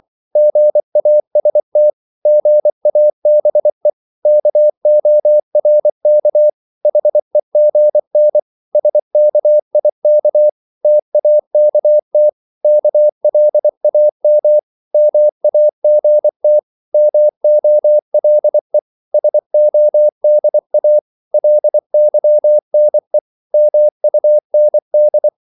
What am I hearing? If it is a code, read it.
Morse code, heard as a keyed tone, about 24 wpm, 600 Hz. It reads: GAST GABE KORK HEGN SKIK TAKT KLAM MAGT MOLE SODA LYNE MUND
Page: Korte danske ord 24wpm